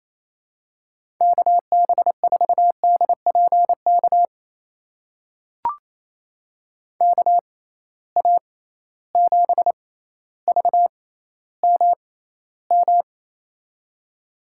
Morse code, heard as a keyed tone, about 27 wpm, spaced as for 14 wpm, 700 Hz. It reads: KB4DPK KA7VMM
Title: USA Call Signs - 6x Character Spacing - 28wpm